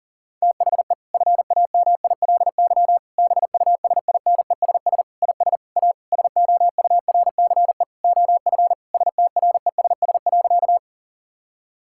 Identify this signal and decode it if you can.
Morse code, heard as a keyed tone, about 40 wpm, 700 Hz. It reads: THE FAMILY BUSINESS IS A SOURCE OF STRESS.
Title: Sentences from Top 1000 Words - Review Part 6 - 40wpm